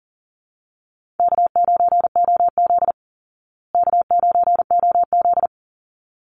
Morse code, heard as a keyed tone, about 40 wpm, 700 Hz. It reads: K9OZ K9OZ
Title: Easy Call Signs Repeated 2x - 40wpm